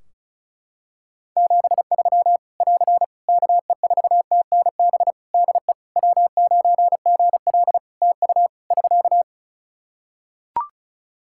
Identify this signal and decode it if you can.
Morse code, heard as a keyed tone, about 36 wpm, 700 Hz. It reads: 73 <AR> KE4TNB DE W9GL TU <SK>
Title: K1USN SST QSO Practice 35wpm